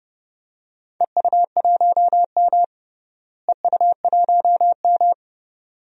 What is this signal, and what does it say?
Morse code, heard as a keyed tone, about 29 wpm, 700 Hz. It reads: EU1M EU1M
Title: DXCC-100 Call Signs - 4 Characters - Repeated 2x - 30wpm